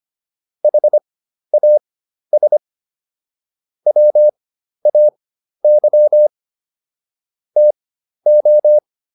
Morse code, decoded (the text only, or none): HAS WAY TO